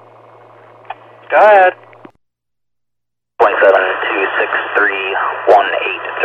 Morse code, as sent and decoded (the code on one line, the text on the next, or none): none